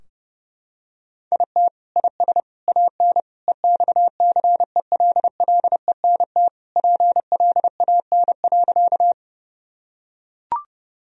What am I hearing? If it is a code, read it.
Morse code, heard as a keyed tone, about 30 wpm, 700 Hz. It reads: IT IS AN EXCELLENT PLAN.
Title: Sentences from Top 1000 Words - Review Part 2 - 30wpm